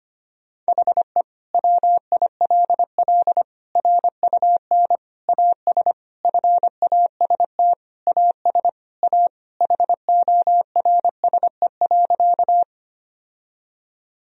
Morse code, heard as a keyed tone, about 25 wpm, 700 Hz. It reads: HE WILL RUN AS FAST AS A HORSE.
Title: Sentences from Top 1000 Words - Review Part 2 - 25wpm